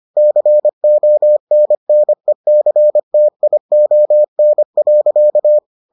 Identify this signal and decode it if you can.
Morse code, heard as a keyed tone, about 25 wpm, 600 Hz. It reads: CONNECTION.